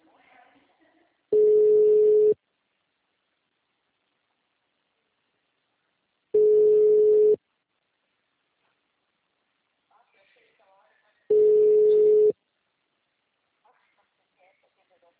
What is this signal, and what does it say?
Ringback tone (Europe, 425 Hz), with speech.